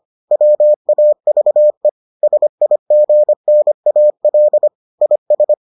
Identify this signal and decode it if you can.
Morse code, heard as a keyed tone, about 25 wpm, 600 Hz. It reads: WAVE SIGNAL IS